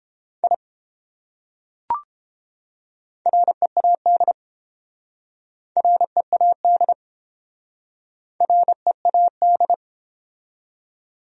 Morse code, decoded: I READ READ READ

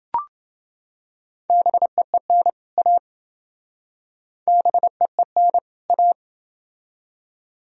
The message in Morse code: BEEN A BEEN A

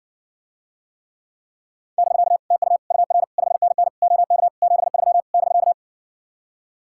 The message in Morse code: <BK> TU UR 5NN OK 73 <BK>

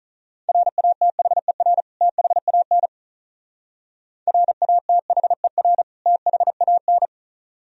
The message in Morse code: RATHER THAN RATHER THAN

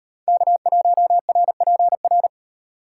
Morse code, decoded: K1RPR

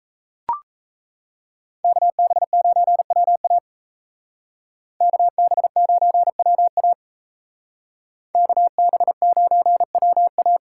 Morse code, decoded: KB9WA KB9WA KB9WA

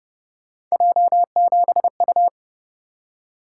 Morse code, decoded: J7U

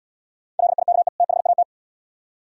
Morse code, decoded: LIKE THAT

